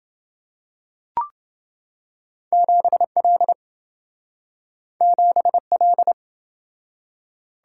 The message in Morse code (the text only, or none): 7L 7L